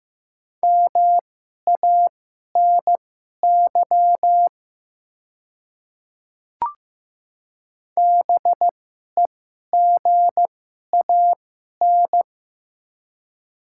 Morse code, heard as a keyed tone, about 15 wpm, spaced as for 11 wpm, 690 Hz. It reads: MANY BEGAN
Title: 2x Character Spacing - Top 500 Words - 15wpm